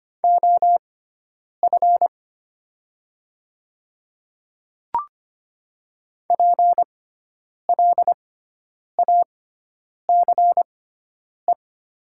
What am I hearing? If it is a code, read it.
Morse code, heard as a keyed tone, about 25 wpm, spaced as for 9 wpm, 700 Hz. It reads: OF PLACE
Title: 6x Character Spacing - Top 200 Words - 25wpm